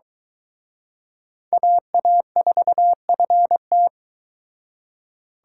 Morse code, heard as a keyed tone, about 23 wpm, 700 Hz. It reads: AA4FT